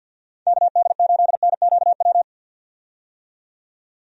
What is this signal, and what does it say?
Morse code, heard as a keyed tone, about 50 wpm, 700 Hz. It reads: KD8NQW
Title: USA Call Signs - 6 Characters - 50wpm